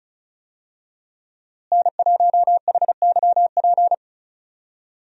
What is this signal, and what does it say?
Morse code, heard as a keyed tone, about 36 wpm, 700 Hz. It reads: N1HYP
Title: USA Call Signs - 5 Characters - Repeated 2x - 35wpm